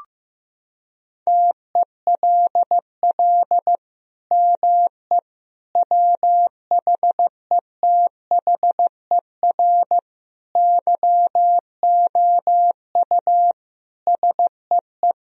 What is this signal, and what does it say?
Morse code, heard as a keyed tone, about 15 wpm, 700 Hz. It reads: TELL ME WHETHER YOU SEE
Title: Sentences from Top 1000 Words - Review Part 3 - 15wpm